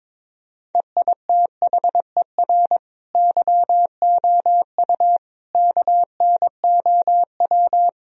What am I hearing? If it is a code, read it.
Morse code, heard as a keyed tone, about 22 wpm, 700 Hz. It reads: EITHER YOU KNOW
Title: Sentences from Top 1000 Words - Review Part 4 - 22wpm